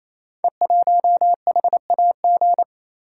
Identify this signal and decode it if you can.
Morse code, heard as a keyed tone, about 27 wpm, 700 Hz. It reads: E1HAG
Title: Code Groups - 5 Letter:Numbers - 28wpm